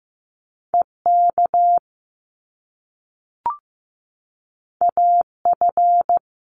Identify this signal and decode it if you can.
Morse code, heard as a keyed tone, about 15 wpm, 700 Hz. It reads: EK AF
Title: Top Two Letter Combinations in Words 15wpm